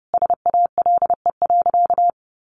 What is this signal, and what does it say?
Morse code, heard as a keyed tone, about 30 wpm, 700 Hz. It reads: SALE.
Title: Sentences from Top 1400 Words 30wpm